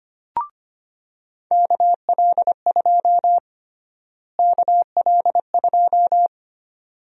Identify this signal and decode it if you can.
Morse code, heard as a keyed tone, about 25 wpm, 700 Hz. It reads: KL2 KL2
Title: USA Call Sign Prefixes - Repeated 2x - 25wpm